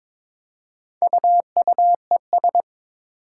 Morse code, heard as a keyed tone, about 22 wpm, 700 Hz. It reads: UUES
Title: Four Challenging Letter-Number Combinations 22wpm